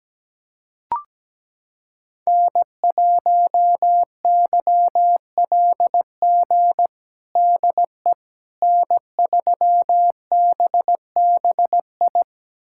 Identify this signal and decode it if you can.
Morse code, heard as a keyed tone, about 17 wpm, 700 Hz. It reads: N1YLG DE N3BBI
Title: USA Callsign DE Callsign - 5 Characters - 17wpm